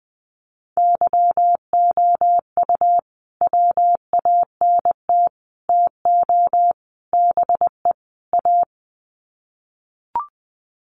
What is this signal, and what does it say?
Morse code, heard as a keyed tone, about 20 wpm, 700 Hz. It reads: YOU WANT TO BE A
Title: Sets of 5 Words Encore2 20wpm